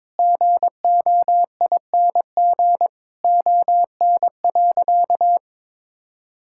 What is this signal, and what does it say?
Morse code, heard as a keyed tone, about 21 wpm, 700 Hz. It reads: GOING ON.
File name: practice / 22wpm / Sentences with 8-Word Phrases - 22wpm